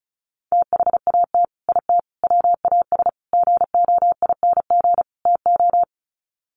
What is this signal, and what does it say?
Morse code, heard as a keyed tone, about 34 wpm, 700 Hz. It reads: THAT IT WAS GOING TO
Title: Sets of 5 Words Encore2 35wpm